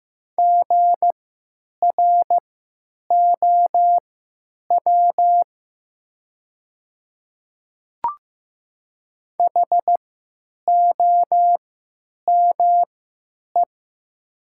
Morse code, heard as a keyed tone, about 15 wpm, spaced as for 9 wpm, 700 Hz. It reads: GROW HOME
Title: 3x Character Spacing - Top 500 Words - 15wpm